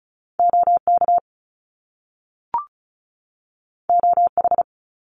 Morse code, heard as a keyed tone, about 36 wpm, 700 Hz. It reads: OK OH
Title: Top Two Letter Combinations in Words 35wpm